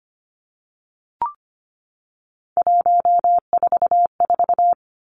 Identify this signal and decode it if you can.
Morse code, heard as a keyed tone, about 25 wpm, 700 Hz. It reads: 144
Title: Three Challenging Letter-Number Combinations 25wpm